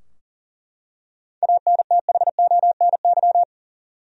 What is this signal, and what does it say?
Morse code, heard as a keyed tone, about 40 wpm, 700 Hz. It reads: ANTHONY
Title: USA Common Names - 40wpm